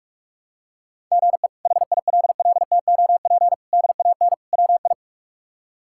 Morse code, heard as a keyed tone, about 43 wpm, 700 Hz. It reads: GE HILLTOP DAN WI
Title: K1USN SST QSO Practice - 45wpm